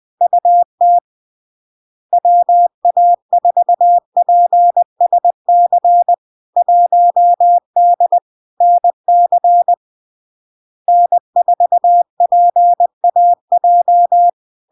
Morse code, heard as a keyed tone, about 20 wpm, 700 Hz. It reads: UT WA4PSC 1D NC N4PAJ
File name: LARK-FD-2022-Log-in-CW-20wpm-700-Hz